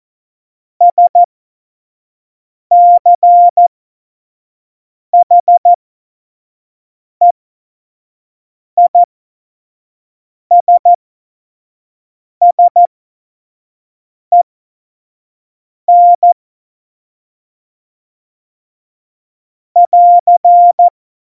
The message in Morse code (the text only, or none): SCHEISSEN <AR>